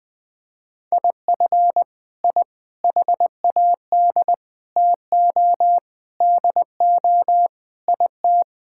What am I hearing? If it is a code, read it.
Morse code, heard as a keyed tone, about 19 wpm, 700 Hz. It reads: IF I HAD TO DO IT
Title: Sets of 9 Words - 20wpm